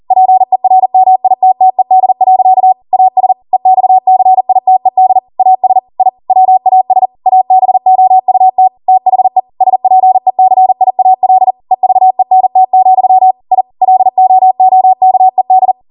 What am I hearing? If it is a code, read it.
Morse code, heard as a keyed tone, about 40 wpm, 750 Hz. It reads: PERMITTED. AS EXCITED AS I WAS ABOUT THE SPECIAL EVENT, I LOOKED